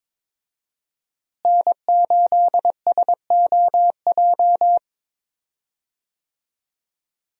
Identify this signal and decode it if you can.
Morse code, heard as a keyed tone, about 21 wpm, 700 Hz. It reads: N8SOJ
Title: USA Call Signs - 5 Characters - Repeated 2x - 22wpm